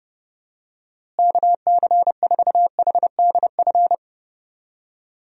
Morse code, heard as a keyed tone, about 30 wpm, 700 Hz. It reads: KC4HDF
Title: USA Call Signs - 6 Characters - Repeated 3x - 30wpm